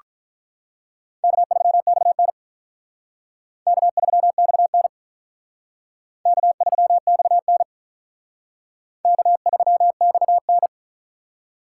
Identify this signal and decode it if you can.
Morse code, heard as a keyed tone, about 43 wpm, 700 Hz. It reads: K3XN K3XN K3XN K3XN
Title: Speed Racing - USA 1x2 Call Signs 35wpm